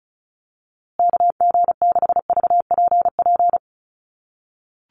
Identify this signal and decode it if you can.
Morse code, heard as a keyed tone, about 34 wpm, 700 Hz. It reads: KG6VPP
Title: USA Call Signs Repeated 3x 35wpm